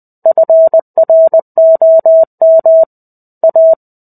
Morse code, heard as a keyed tone, about 20 wpm, 650 Hz. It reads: FROM A
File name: 05-Chapter-05 20 WPM